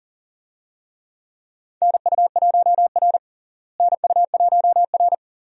Morse code, heard as a keyed tone, about 40 wpm, 700 Hz. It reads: NU1R NU1R
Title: USA Call Signs - 4 Characters - Repeated 2x - 40wpm